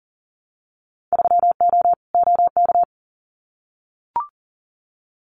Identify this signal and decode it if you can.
Morse code, heard as a keyed tone, about 41 wpm, 700 Hz. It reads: 3O OK
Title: Winter Field Day Exchanges 40wpm